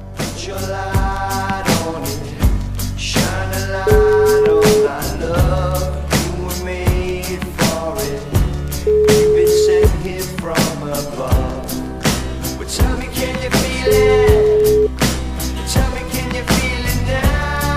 Ringback tone (Europe, 425 Hz), with noise.